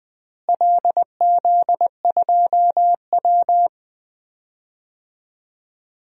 Morse code, heard as a keyed tone, about 20 wpm, 700 Hz. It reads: LZ2W
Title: DXCC-100 Call Signs - 4 Characters - Repeated 2x - 20wpm